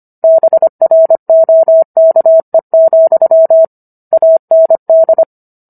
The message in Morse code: BROKE, AND

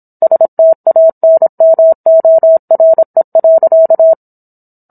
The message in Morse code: STANMORE.

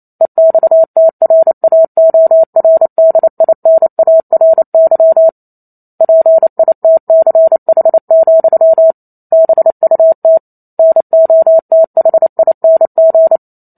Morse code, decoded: EXTRAORDINARY PITCH, BUT NOTHING

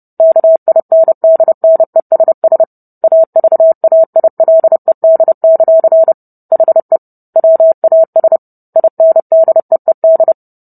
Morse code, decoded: KINDNESS AVAILED; HE WAS INDEED

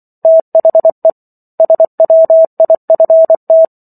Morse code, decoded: THE SWIFT

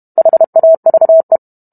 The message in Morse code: HAVE

